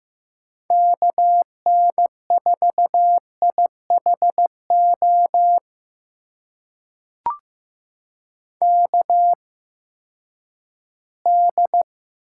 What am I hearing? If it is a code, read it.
Morse code, heard as a keyed tone, about 15 wpm, spaced as for 10 wpm, 700 Hz. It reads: KN4IHO KD
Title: USA Call Signs - 6 Characters - 8x Character Spacing - 15wpm